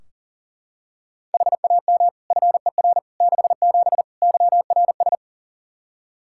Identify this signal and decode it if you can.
Morse code, heard as a keyed tone, about 40 wpm, 700 Hz. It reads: HAM FER 67 YRS
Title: Practice QSO - Exchange 3 - More Info - 40wpm